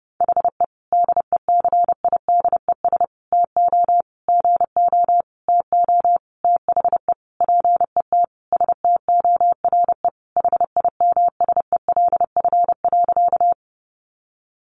Morse code, heard as a keyed tone, about 30 wpm, 700 Hz. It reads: HE DECIDES TO GO TO THE PET STORE HIMSELF.
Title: Easy Phrases 30wpm